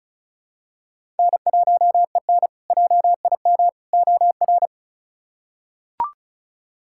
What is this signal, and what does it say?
Morse code, heard as a keyed tone, about 34 wpm, 700 Hz. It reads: N1EN JIM OR
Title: K1USN SST QSO Practice - 35wpm